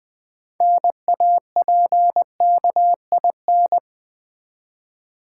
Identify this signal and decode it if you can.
Morse code, heard as a keyed tone, about 20 wpm, 700 Hz. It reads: NAPKIN
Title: Easy Words with Hints - 20wpm